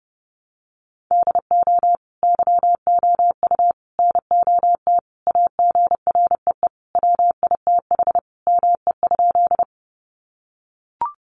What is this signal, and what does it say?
Morse code, heard as a keyed tone, about 28 wpm, 700 Hz. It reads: DO YOU NOT AGREE WITH ME?
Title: Sentences from Top 1100 Words 30wpm